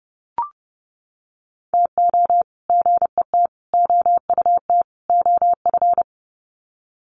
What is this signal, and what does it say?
Morse code, heard as a keyed tone, about 30 wpm, 700 Hz. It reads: TO GET OUT OF
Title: Sets of 4 Words 30wpm